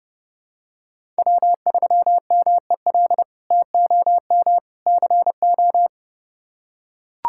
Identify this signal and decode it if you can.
Morse code, heard as a keyed tone, about 30 wpm, 700 Hz. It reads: W3MEL TOM CO
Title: K1USN SST QSO Practice - 30wpm